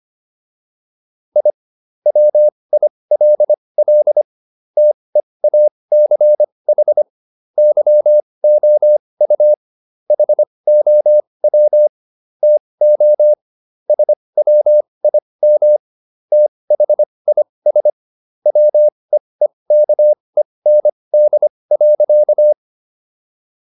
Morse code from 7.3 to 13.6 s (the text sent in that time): YOU HOW TO